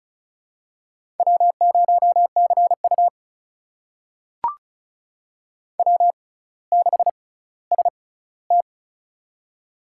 Morse code, heard as a keyed tone, about 36 wpm, spaced as for 18 wpm, 700 Hz. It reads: W0CU W6ST
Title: USA Call Signs - 4 Characters - 6x Character Spacing - 35wpm